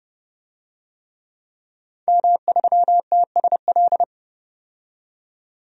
Morse code, heard as a keyed tone, about 30 wpm, 700 Hz. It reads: M3TSL
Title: DXCC-100 Call Signs - 5 Characters - 30wpm